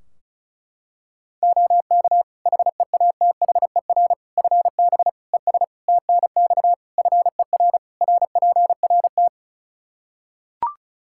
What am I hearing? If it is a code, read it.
Morse code, heard as a keyed tone, about 36 wpm, 700 Hz. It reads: OK HEATHER FB ES TNX FER RPRT
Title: K1USN SST QSO Practice 35wpm